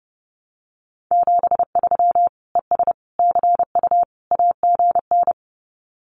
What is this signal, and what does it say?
Morse code, heard as a keyed tone, about 30 wpm, 700 Hz. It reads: 73 ES CU AGN
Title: Common QSO Phrases Encore 30wpm